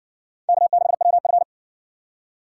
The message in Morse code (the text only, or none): K6WF